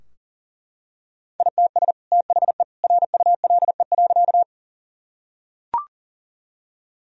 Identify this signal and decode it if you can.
Morse code, heard as a keyed tone, about 40 wpm, 700 Hz. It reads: ITS THE RULE.
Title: Sentences from Top 1000 Words - Review Part 2 - 40wpm